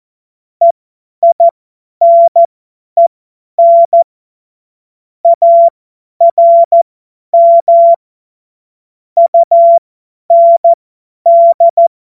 Morse code, heard as a keyed tone, about 14 wpm, spaced as for 10 wpm, 680 Hz. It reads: EINEN ARM UND
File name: Morsekurs / Witze / Witz 01. August 2023 50 F70